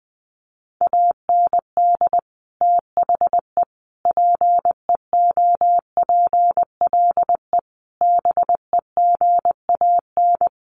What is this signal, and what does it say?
Morse code, heard as a keyed tone, about 20 wpm, 700 Hz. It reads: AND THE PEOPLE BEGAN
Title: Sentences from Top 1500 Words 20wpm